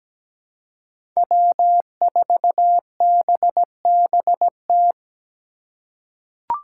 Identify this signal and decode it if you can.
Morse code, heard as a keyed tone, about 17 wpm, 700 Hz. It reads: W4BBT
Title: K1USN SST QSO Practice - 17wpm